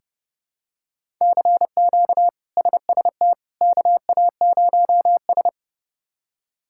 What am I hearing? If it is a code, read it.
Morse code, heard as a keyed tone, about 29 wpm, 700 Hz. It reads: CQ SST KA0S
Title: K1USN SST QSO Practice - 30wpm